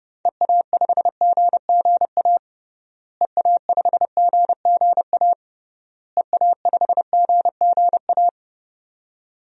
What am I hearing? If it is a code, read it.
Morse code, heard as a keyed tone, about 30 wpm, 700 Hz. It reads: EA5GGA EA5GGA EA5GGA
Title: DXCC-100 Call Signs - 6 Characters - Repeated 3x - 30wpm